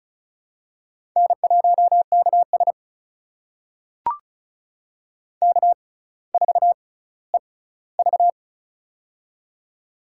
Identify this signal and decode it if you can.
Morse code, heard as a keyed tone, about 36 wpm, spaced as for 18 wpm, 700 Hz. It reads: N1KS K4EV
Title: USA Call Signs - 4 Characters - 6x Character Spacing - 35wpm